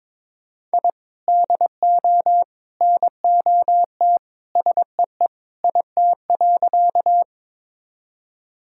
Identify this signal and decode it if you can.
Morse code, heard as a keyed tone, about 22 wpm, 700 Hz. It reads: I DO NOT SEE IT.
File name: practice / 22wpm / Sentences with 3-Word Phrases - 22wpm